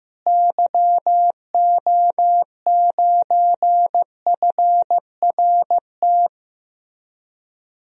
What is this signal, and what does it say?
Morse code, heard as a keyed tone, about 15 wpm, 700 Hz. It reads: YO9FRT
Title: DXCC-100 Call Signs - 6 Characters - Repeated 3x - 15wpm